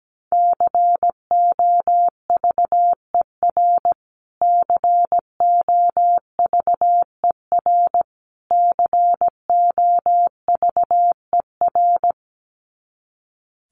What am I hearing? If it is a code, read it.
Morse code, heard as a keyed tone, about 17 wpm, 700 Hz. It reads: COVER COVER COVER
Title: Top 300 Words Repeated 17wpm